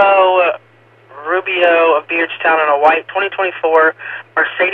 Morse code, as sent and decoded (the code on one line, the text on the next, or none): none